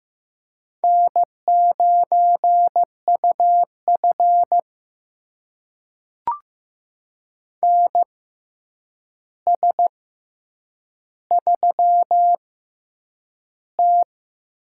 Morse code, heard as a keyed tone, about 15 wpm, spaced as for 8 wpm, 700 Hz. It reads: N9UF NS3T
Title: USA Call Signs - 4 Characters - 6x Character Spacing - 15wpm